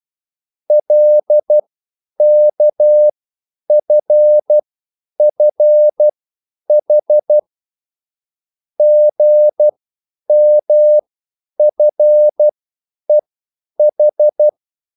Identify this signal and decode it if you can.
Morse code, heard as a keyed tone, about 12 wpm, spaced as for 9 wpm, 600 Hz. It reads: LKFFH GMFEH